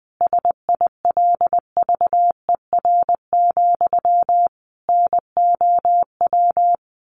Morse code, heard as a keyed tone, about 20 wpm, 700 Hz. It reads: SILVER, NOW
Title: A Sound Of Thunder 20wpm